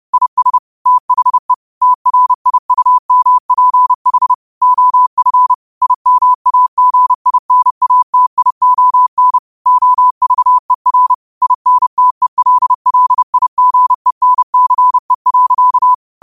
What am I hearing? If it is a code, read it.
Morse code, heard as a keyed tone, about 30 wpm, 1000 Hz. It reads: IS THE TRIUMPH OF IMAGINATION OVER INTELLIGENCE.
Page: Quotes for Fri, 15 Aug 2025 in Morse Code at 30 words per minute.